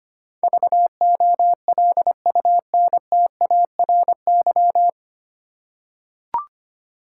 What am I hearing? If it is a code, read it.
Morse code, heard as a keyed tone, about 25 wpm, 700 Hz. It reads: VOLUNTARY
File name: practice / 25wpm / 9 Letter Words - First 3 Letters Given - 25wpm